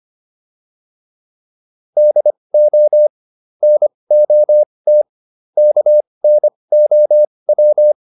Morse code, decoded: DO NOT KNOW

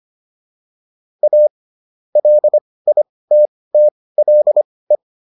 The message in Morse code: A LITTLE

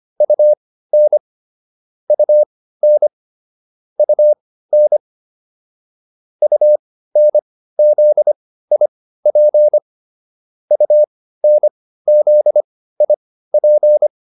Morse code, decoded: UN UN UN UNZIP UNZIP

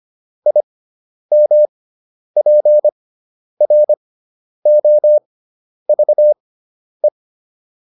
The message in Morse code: IMPROVE

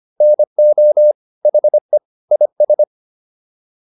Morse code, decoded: NO HE IS

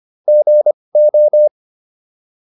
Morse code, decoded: GO

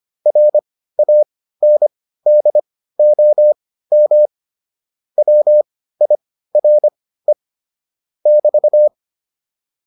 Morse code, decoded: RANDOM WIRE <BT>

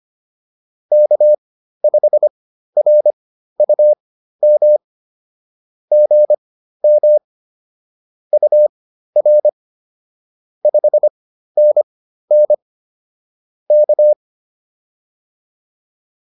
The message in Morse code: K5RUM GM UR 5NN K

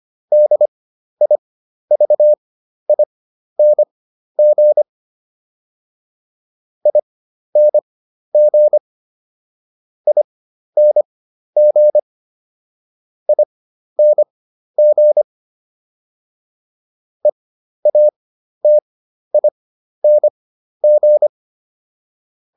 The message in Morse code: DIVING ING ING ING EATING

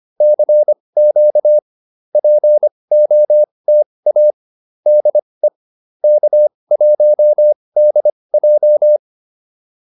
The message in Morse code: CQ POTA DE K1DJ

